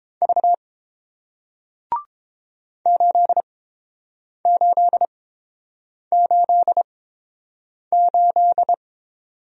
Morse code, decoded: V 8888

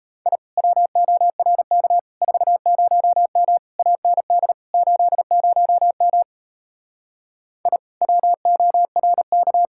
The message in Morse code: I WORK 40M AND 80M I WORK

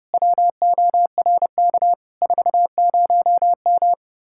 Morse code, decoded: WORK 40M

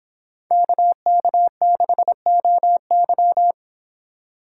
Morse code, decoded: KK6OY

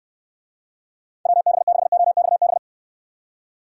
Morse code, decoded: WB6OXB

